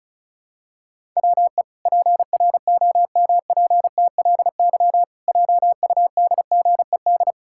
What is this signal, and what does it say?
Morse code, heard as a keyed tone, about 34 wpm, 700 Hz. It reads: WE PROMPTLY JUDGED
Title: Pangram Challenge v2 - 35wpm